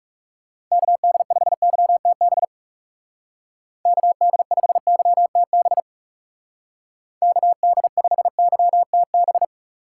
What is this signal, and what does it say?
Morse code, heard as a keyed tone, about 38 wpm, 700 Hz. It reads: KD5YTB KD5YTB KD5YTB